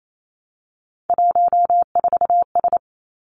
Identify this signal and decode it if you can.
Morse code, heard as a keyed tone, about 28 wpm, 700 Hz. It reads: 14S
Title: Three Challenging Letter-Number Combinations 28wpm